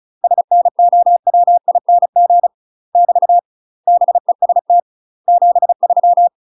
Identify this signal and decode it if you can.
Morse code, heard as a keyed tone, about 36 wpm, 700 Hz. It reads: SNOWING <BT> BEST 73
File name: UK MASTER QSOs TRAINING / 35WPM / UK MASTER QSOs 0038